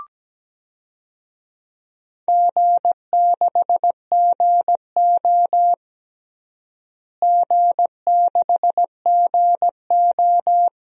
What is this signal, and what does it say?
Morse code, heard as a keyed tone, about 17 wpm, 700 Hz. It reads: G6GO G6GO
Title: DXCC-100 Call Signs - 4 Characters - Repeated 3x - 17wpm